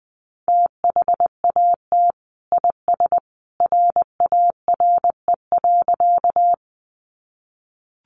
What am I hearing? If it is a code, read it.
Morse code, heard as a keyed tone, about 20 wpm, 700 Hz. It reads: THAT IS RARE.
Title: Sentences from Top 1100 Words 20wpm